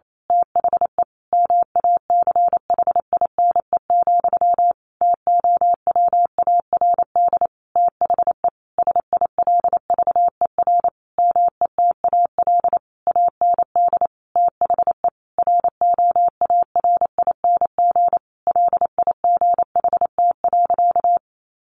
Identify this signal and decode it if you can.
Morse code, heard as a keyed tone, about 28 wpm, 700 Hz. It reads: THE MACHINE, TOWARD THE SILVER METAL AND THE ROARING LIGHT.
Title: A Sound Of Thunder 28wpm